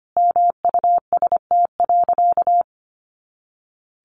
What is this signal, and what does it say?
Morse code, heard as a keyed tone, about 25 wpm, 700 Hz. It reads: MUST.
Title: Sentences from Top 200 Words 25wpm